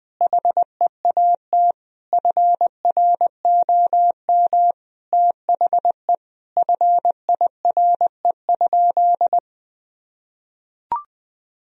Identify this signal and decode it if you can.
Morse code, heard as a keyed tone, about 20 wpm, 700 Hz. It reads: HEAT FROM THE FIRE?
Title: Sentences from Top 1000 Words - Review Part 3 - 20wpm